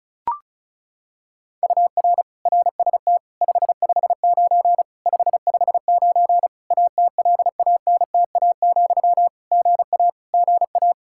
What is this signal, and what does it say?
Morse code, heard as a keyed tone, about 34 wpm, 700 Hz. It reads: UR RST 559 559 ATLANTA, GA GA
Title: K1USN SST QSO Practice 35wpm